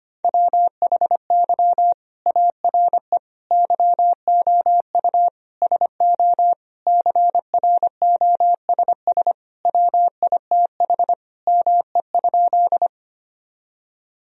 Morse code, heard as a keyed tone, about 25 wpm, 700 Hz. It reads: WHY ARE YOU SO CROSS WITH ME?
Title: Sentences from Top 1000 Words - Review Part 1 - 25wpm